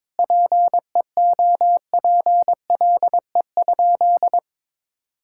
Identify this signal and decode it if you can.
Morse code, heard as a keyed tone, about 22 wpm, 700 Hz. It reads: PEOPLE?
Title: Sentences from Top 1000 Words - Review Part 6 - 22wpm